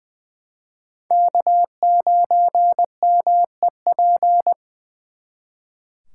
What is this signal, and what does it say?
Morse code, heard as a keyed tone, about 24 wpm, 700 Hz. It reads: K9MEP
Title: Speed Racing - USA 1x3 Call Signs 20wpm